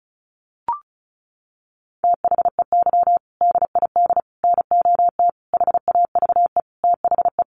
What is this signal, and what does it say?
Morse code, heard as a keyed tone, about 34 wpm, 700 Hz. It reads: THEY DID NOT HAVE THE
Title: Sentences from Top 1400 Words 35wpm